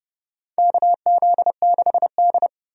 Morse code, heard as a keyed tone, about 30 wpm, 700 Hz. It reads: KZ6D
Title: USA Call Signs - 4 Characters - 30wpm